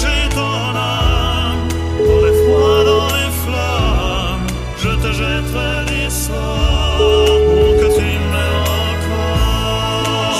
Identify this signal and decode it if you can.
Ringback tone (Europe, 425 Hz), with noise.